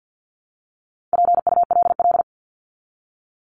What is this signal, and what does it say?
Morse code, heard as a keyed tone, about 50 wpm, 700 Hz. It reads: PULL
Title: Top 500 Words 50wpm